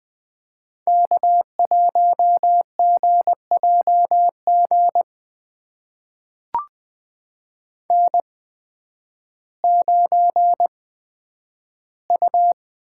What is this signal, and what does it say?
Morse code, heard as a keyed tone, about 20 wpm, spaced as for 11 wpm, 700 Hz. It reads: K1GJG N9U
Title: USA Call Signs - 5 Characters - 8x Character Spacing - 20wpm